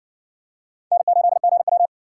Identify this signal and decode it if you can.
Morse code, heard as a keyed tone, about 60 wpm, 700 Hz. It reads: N8GW